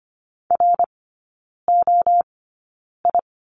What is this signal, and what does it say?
Morse code, heard as a keyed tone, about 25 wpm, spaced as for 9 wpm, 700 Hz. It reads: ROI
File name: Lesson 14e - Number 2 - Three Characters 25wpm